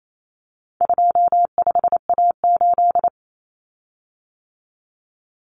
Morse code, heard as a keyed tone, about 28 wpm, 700 Hz. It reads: 25A8
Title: Common Cut Numbers - 4-Digit 28wpm